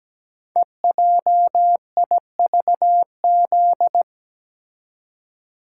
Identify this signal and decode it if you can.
Morse code, heard as a keyed tone, about 17 wpm, 700 Hz. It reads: EJIVZ
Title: Code Groups - 5 Letters - 17wpm